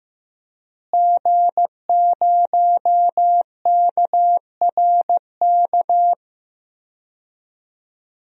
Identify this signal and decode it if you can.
Morse code, heard as a keyed tone, about 16 wpm, 700 Hz. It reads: G0KRK
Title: DXCC-100 Call Signs - 5 Characters - Repeated 3x - 15wpm